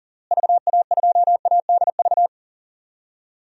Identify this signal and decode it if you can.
Morse code, heard as a keyed tone, about 40 wpm, 700 Hz. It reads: VA2ADV
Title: DXCC-100 Call Signs - 6 Characters - Repeated 3x - 40wpm